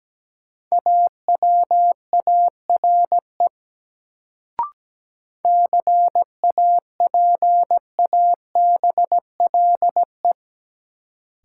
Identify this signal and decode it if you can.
Morse code, heard as a keyed tone, about 17 wpm, 700 Hz. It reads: AWARE CAPABLE
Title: Sentences from Top 1200 Words - New Word Drill Alt - 17wpm